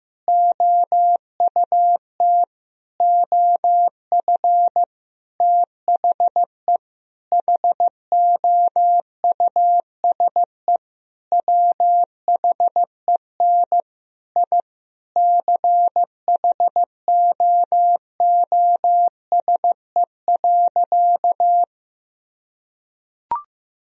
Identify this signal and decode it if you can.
Morse code, heard as a keyed tone, about 15 wpm, 700 Hz. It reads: OUT OF THE HOUSE WHEN I CHOOSE.
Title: Sentences from Top 1000 Words - Review Part 4 - 15wpm